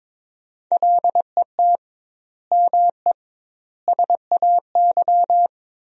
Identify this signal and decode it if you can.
Morse code, heard as a keyed tone, about 22 wpm, 700 Hz. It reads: LET ME SAY
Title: 2x Word-Spacing - Sentences from Top 300 Words - 22wpm